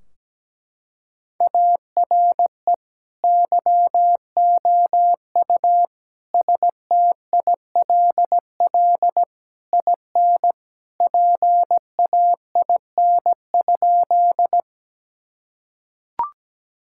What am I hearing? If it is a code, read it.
Morse code, heard as a keyed tone, about 17 wpm, 700 Hz. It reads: ARE YOU STILL IN PAIN?
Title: Sentences from Top 1600 Words 17wpm